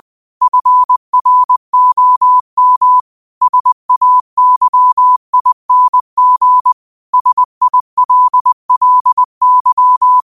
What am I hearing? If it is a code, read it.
Morse code, heard as a keyed tone, about 20 wpm, 1000 Hz. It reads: FROM SAYING SILLY